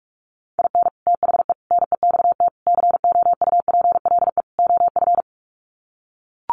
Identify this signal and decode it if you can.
Morse code, heard as a keyed tone, about 44 wpm, 700 Hz. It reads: IN THE NEXT COUPLE OF
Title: Sets of 5 Words Encore2 45wpm